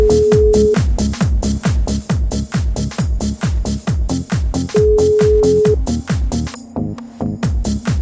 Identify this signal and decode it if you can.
Ringback tone (Europe, 425 Hz), with noise.